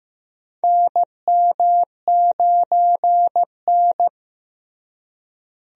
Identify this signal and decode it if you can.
Morse code, heard as a keyed tone, about 15 wpm, 700 Hz. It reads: NM9N
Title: USA Call Signs - 4 Characters - Repeated 2x - 15wpm